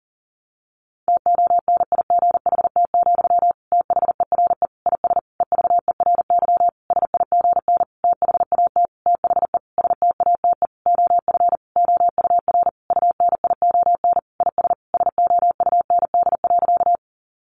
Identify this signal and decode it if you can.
Morse code, heard as a keyed tone, about 37 wpm, 700 Hz. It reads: TONIGHT, THERE IS EVERY SIGN THAT THE STATE OF OUR UNION IS SOUND.
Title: Sentences from Top 1400 Words 40wpm